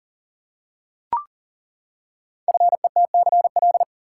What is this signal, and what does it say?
Morse code, heard as a keyed tone, about 38 wpm, 700 Hz. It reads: FETCL
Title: Code Groups - 5 Letters - 40wpm